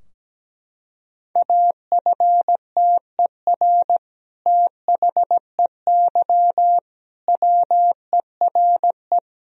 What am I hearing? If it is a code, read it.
Morse code, heard as a keyed tone, about 17 wpm, 700 Hz. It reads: AFTER THEY WERE
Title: Sentences from Top 1000 Words - Review Part 2 - 17wpm